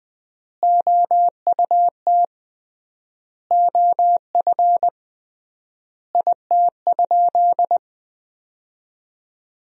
Morse code, heard as a keyed tone, about 21 wpm, 700 Hz. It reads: OUT OF IT?
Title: 3x Word-Spacing - Sentences from Top 300 Words - 20wpm